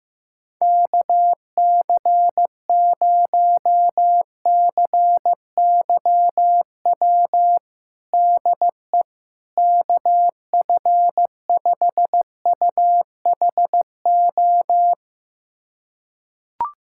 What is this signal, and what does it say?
Morse code, heard as a keyed tone, about 16 wpm, 700 Hz. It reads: KC0CYW DE KF5UHO
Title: USA Callsign DE Callsign - 6 Characters - 15wpm